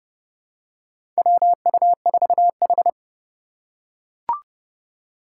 Morse code, heard as a keyed tone, about 32 wpm, 700 Hz. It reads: WU4H
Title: USA Call Signs - 2x Character Spacing - 30wpm